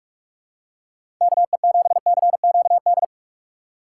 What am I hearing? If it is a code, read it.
Morse code, heard as a keyed tone, about 46 wpm, 700 Hz. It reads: KE7CQD
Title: USA Call Signs - 6 Characters - 3x Character Spacing - 45wpm